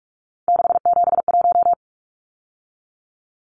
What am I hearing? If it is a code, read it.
Morse code, heard as a keyed tone, about 44 wpm, 700 Hz. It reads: 671
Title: Numbers - 3-Digit 45wpm